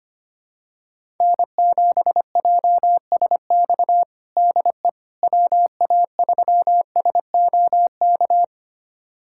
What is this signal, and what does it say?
Morse code, heard as a keyed tone, about 25 wpm, 700 Hz. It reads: N7JSX DE WA3SOK
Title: USA Callsign DE Callsign 25wpm